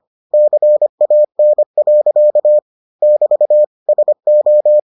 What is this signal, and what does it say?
Morse code, heard as a keyed tone, about 25 wpm, 600 Hz. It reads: CAN. <BT> SO